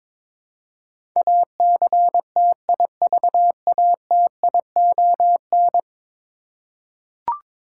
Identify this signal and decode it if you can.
Morse code, heard as a keyed tone, about 22 wpm, 700 Hz. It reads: ACTIVATION
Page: morse-code-ninja-practice-22wpm